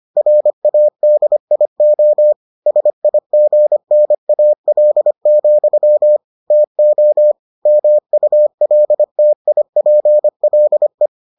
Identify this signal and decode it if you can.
Morse code, heard as a keyed tone, about 25 wpm, 600 Hz. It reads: RADIO SIGNAL, TO MULTIPLE